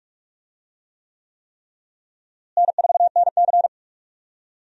Morse code, heard as a keyed tone, about 43 wpm, 700 Hz. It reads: N4NC
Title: USA Call Signs - 4 Characters - Repeated 3x - 45wpm